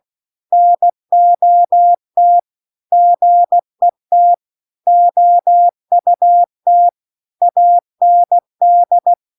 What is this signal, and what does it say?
Morse code, heard as a keyed tone, about 16 wpm, 700 Hz. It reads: NOT GET OUT AND
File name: CW-The-Big-Twoheaded-River-2-16WPM